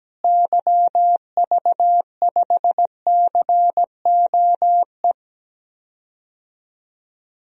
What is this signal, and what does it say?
Morse code, heard as a keyed tone, about 17 wpm, 700 Hz. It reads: YV5COE
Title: DXCC-100 Call Signs - 6 Characters - Repeated 3x - 17wpm